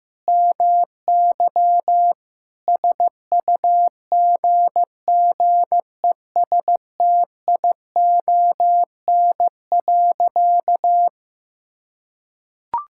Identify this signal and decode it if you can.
Morse code, heard as a keyed tone, about 15 wpm, 700 Hz. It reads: MY SUGGESTION.
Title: Sentences from Top 1600 Words 15wpm